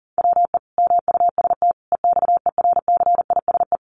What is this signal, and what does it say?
Morse code, heard as a keyed tone, about 40 wpm, 700 Hz. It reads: WE MUST EXERCISE
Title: Sentences from Top 1200 Words 40wpm